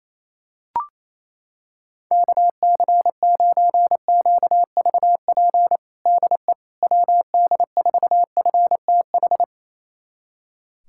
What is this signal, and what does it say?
Morse code, heard as a keyed tone, about 28 wpm, 700 Hz. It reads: KC9QVP DE WD4FTH
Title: USA Callsign DE Callsign - 6 Characters - 28wpm